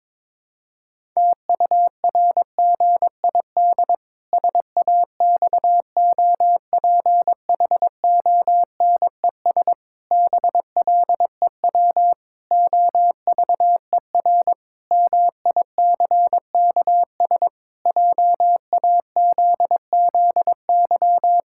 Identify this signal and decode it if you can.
Morse code, heard as a keyed tone, about 22 wpm, 700 Hz. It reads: TURGID SAXOPHONES BLEW OVER MICKS JAZZY
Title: Pangram Challenge v2 - 22wpm